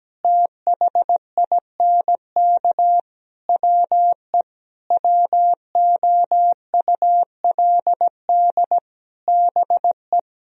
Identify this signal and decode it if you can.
Morse code, heard as a keyed tone, about 17 wpm, 700 Hz. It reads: THINK WE WOULD BE
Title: Sentences from Top 1000 Words - Review Part 3 - 17wpm